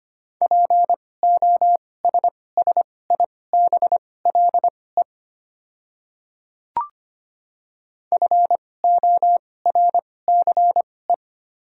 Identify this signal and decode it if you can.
Morse code, heard as a keyed tone, about 25 wpm, spaced as for 18 wpm, 700 Hz. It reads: POSSIBLE FORCE
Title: 2x Character Spacing - Top 500 Words - 25wpm